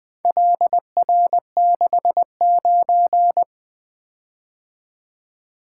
Morse code, heard as a keyed tone, about 20 wpm, 700 Hz. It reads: LR69
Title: Four Letter-Number Combinations - 20wpm